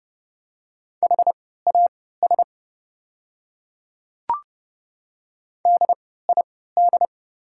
Morse code, heard as a keyed tone, about 30 wpm, spaced as for 17 wpm, 700 Hz. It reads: HAS DID
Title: 3x Character Spacing - Top 100 Words - 30wpm